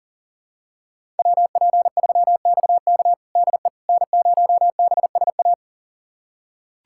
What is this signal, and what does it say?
Morse code, heard as a keyed tone, about 40 wpm, 700 Hz. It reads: WP3XK DE N0BSA
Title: USA Callsign DE Callsign - 5 Characters - 40wpm